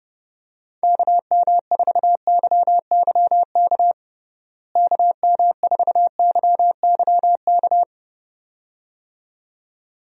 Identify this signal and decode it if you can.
Morse code, heard as a keyed tone, about 30 wpm, 700 Hz. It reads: KM4YYK KM4YYK
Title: USA Call Signs - 6 Characters - Repeated 2x - 30wpm